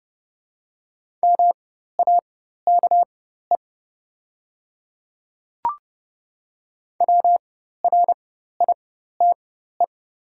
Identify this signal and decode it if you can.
Morse code, heard as a keyed tone, about 30 wpm, spaced as for 14 wpm, 690 Hz. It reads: MAKE WRITE
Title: 4x Character Spacing - Top 100 Words - 30wpm